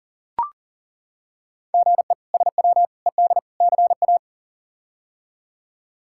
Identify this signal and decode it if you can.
Morse code, heard as a keyed tone, about 40 wpm, 700 Hz. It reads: GE SW ED CA
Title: K1USN SST QSO Practice - 40wpm